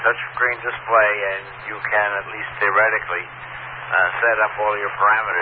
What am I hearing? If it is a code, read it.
Emergency Alert System (SAME) tones, heard as harsh buzzing bursts, with speech. It is heard from 0.3 to 4.4 s.